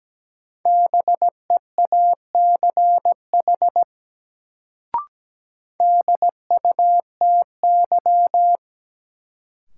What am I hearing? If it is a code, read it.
Morse code, heard as a keyed tone, about 17 wpm, 700 Hz. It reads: BEACH DUTY
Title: Sentences from Top 1500 Words - New Word Drill - 17wpm